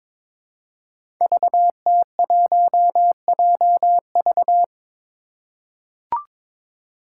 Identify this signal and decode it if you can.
Morse code, heard as a keyed tone, about 22 wpm, 700 Hz. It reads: VT1JV
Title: Five Challenging Letter-Number Combinations 22wpm